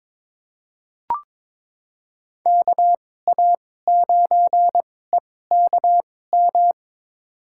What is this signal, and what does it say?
Morse code, heard as a keyed tone, about 22 wpm, spaced as for 16 wpm, 700 Hz. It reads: KA9EKM
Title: USA Call Signs - 6 Characters - 2x Character Spacing - 22wpm